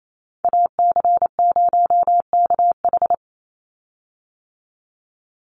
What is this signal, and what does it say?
Morse code, heard as a keyed tone, about 28 wpm, 700 Hz. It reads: AC0KH
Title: USA Call Signs 28wpm